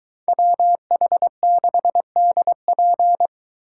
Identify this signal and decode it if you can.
Morse code, heard as a keyed tone, about 22 wpm, 700 Hz. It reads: WH6DP